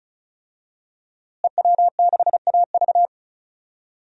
Morse code, heard as a keyed tone, about 34 wpm, 700 Hz. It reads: EW6AV
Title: DXCC-100 Call Signs - 5 Characters - Repeated 2x - 35wpm